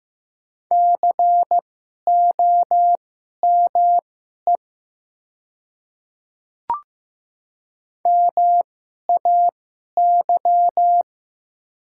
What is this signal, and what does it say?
Morse code, heard as a keyed tone, about 15 wpm, spaced as for 11 wpm, 700 Hz. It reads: COME MAY
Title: 2x Character Spacing - Top 100 Words - 15wpm